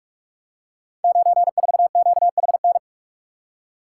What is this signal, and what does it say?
Morse code, heard as a keyed tone, about 44 wpm, 700 Hz. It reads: 94QHN
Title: Code Groups - 5 Letter:Numbers - 45wpm